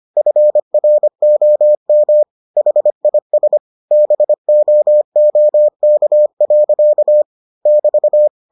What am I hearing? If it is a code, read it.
Morse code, heard as a keyed tone, about 25 wpm, 600 Hz. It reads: FROM HIS BOOK. <BT>